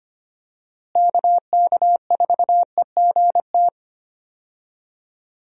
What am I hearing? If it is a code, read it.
Morse code, heard as a keyed tone, about 25 wpm, 700 Hz. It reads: KK4EGT
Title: USA Call Signs - 6 Characters - Repeated 3x - 25wpm